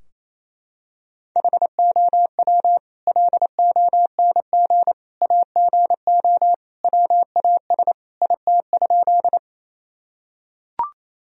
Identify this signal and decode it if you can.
Morse code, heard as a keyed tone, about 28 wpm, 700 Hz. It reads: HOW LONG AGO WAS IT?
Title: Sentences from Top 1000 Words - Review Part 3 - 28wpm